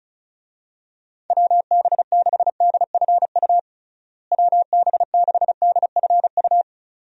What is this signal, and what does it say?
Morse code, heard as a keyed tone, about 36 wpm, 700 Hz. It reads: WB6DFU WB6DFU
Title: USA Call Signs - 6 Characters - Repeated 2x - 35wpm